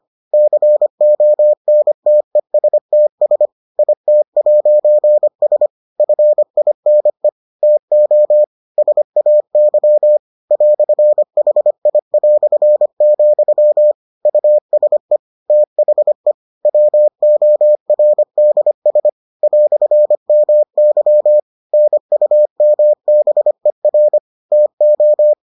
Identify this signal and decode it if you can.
Morse code, heard as a keyed tone, about 25 wpm, 600 Hz. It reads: CONTESTS IT'S FINE TO SAY "HI", USE THE WORDS "MY NUMBER TO